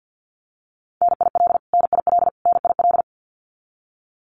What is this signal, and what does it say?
Morse code, heard as a keyed tone, about 50 wpm, 700 Hz. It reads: NIL NIL NIL
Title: Abbreviations Repeated 3x - 50wpm